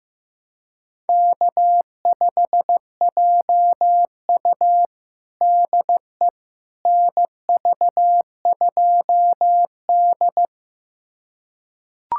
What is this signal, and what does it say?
Morse code, heard as a keyed tone, about 15 wpm, 700 Hz. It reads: K5JU DE NV2D
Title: USA Callsign DE Callsign - 4 Characters - 15wpm